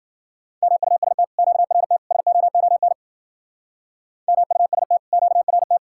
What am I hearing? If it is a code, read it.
Morse code, heard as a keyed tone, about 50 wpm, 700 Hz. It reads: MUST QRT SOON MUST QRT
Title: Speed-Racing Common QSO Phrases Encore - 40wpm